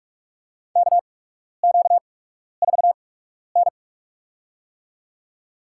Morse code, heard as a keyed tone, about 45 wpm, spaced as for 12 wpm, 700 Hz. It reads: KQ4N
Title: USA Call Signs - 4 Characters - 8x Character Spacing - 45wpm